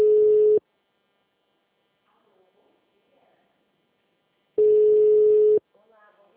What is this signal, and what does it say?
Ringback tone (Europe, 425 Hz), with speech.
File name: out- -2015- - - .19371